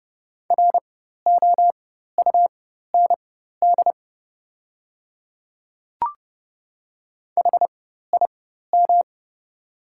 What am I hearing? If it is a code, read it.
Morse code, heard as a keyed tone, about 30 wpm, spaced as for 14 wpm, 700 Hz. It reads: ROUND HIM
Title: 4x Character Spacing - Top 200 Words - 30wpm